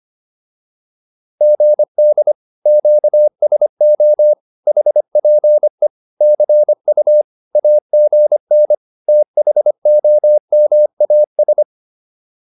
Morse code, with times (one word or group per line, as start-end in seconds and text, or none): 1.4-2.3 GD
2.7-4.3 QSO
4.7-5.9 HPE
6.2-7.2 CU
7.6-8.7 AGN
9.1-11.6 THOMAS